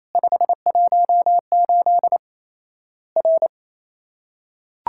Morse code, heard as a keyed tone, about 28 wpm, 700 Hz. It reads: 518 R
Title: Worked All Europe DX Contest - QTC Exchange - 28wpm